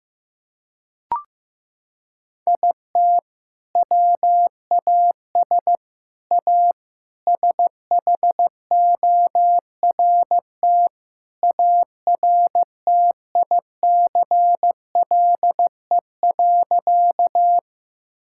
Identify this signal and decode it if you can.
Morse code, heard as a keyed tone, about 14 wpm, 700 Hz. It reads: IT WAS A SHORT ARTICLE.
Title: Sentences from Top 1000 Words - Review Part 5 - 15wpm